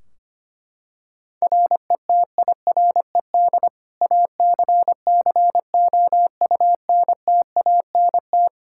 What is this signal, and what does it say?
Morse code, heard as a keyed tone, about 25 wpm, 700 Hz. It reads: RETIRED ACCOUNTANT
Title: K1USN SST QSO Practice 25wpm